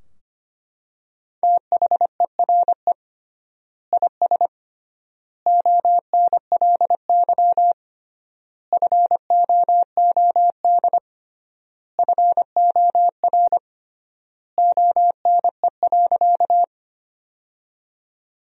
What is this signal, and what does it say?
Morse code, heard as a keyed tone, about 25 wpm, 700 Hz. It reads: THERE IS ONLY FOOD FOR ONE.
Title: 3x Word-Spacing - Sentences from Top 300 Words - 25wpm